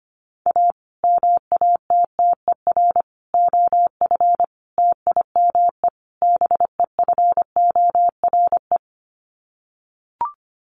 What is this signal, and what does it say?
Morse code, heard as a keyed tone, about 25 wpm, 700 Hz. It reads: A MATTER OF TIME BEFORE
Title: Sets of 5 Words Encore 25wpm